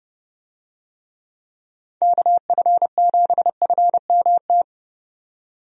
Morse code, heard as a keyed tone, about 30 wpm, 700 Hz. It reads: KF7FMT
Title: USA Call Signs - 6 Characters - 30wpm